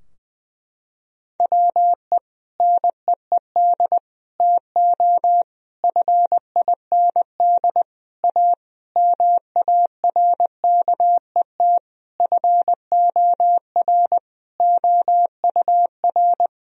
Morse code, heard as a keyed tone, about 20 wpm, 700 Hz. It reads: WE NEED TO FIND A MARKET FOR OUR
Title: Sentences from Top 1000 Words - Review Part 3 - 20wpm